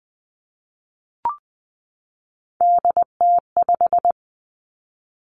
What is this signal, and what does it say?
Morse code, heard as a keyed tone, about 20 wpm, 700 Hz. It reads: DT5
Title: Three Challenging Letter-Number Combinations 20wpm